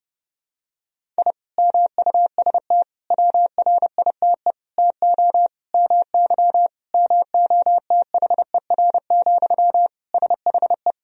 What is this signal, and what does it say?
Morse code, heard as a keyed tone, about 30 wpm, 700 Hz. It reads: I MUST WRITE TO MY MOTHER, SHE
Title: Sentences from Top 1000 Words - Review Part 1 - 30wpm